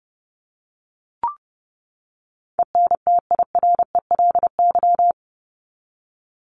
Morse code, heard as a keyed tone, about 30 wpm, 700 Hz. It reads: ENTIRELY
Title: LY Suffix - Top 100 Words with Base Words 30wpm